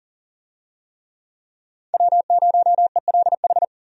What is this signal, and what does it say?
Morse code, heard as a keyed tone, about 37 wpm, 700 Hz. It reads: W0ELH
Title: USA Call Signs - 5 Characters - 40wpm